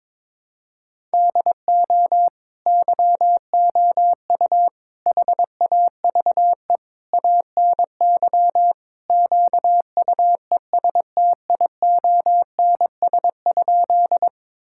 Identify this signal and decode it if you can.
Morse code, heard as a keyed tone, about 22 wpm, 700 Hz. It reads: DO YOU HAVE ANY QUESTIONS?
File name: practice / 22wpm / Sentences with 4-Word Phrases - 22wpm